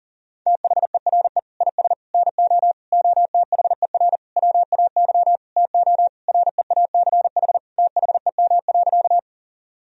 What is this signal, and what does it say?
Morse code, heard as a keyed tone, about 37 wpm, 700 Hz. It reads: THERE IS NO OTHER WAY TO REACH THEM.
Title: Sentences from Top 1000 Words - Review Part 2 - 40wpm